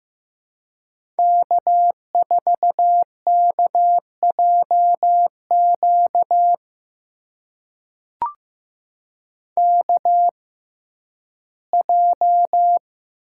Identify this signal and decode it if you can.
Morse code, heard as a keyed tone, about 15 wpm, spaced as for 11 wpm, 700 Hz. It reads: K4KJQ KJ
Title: USA Call Signs - 6x Character Spacing - 15wpm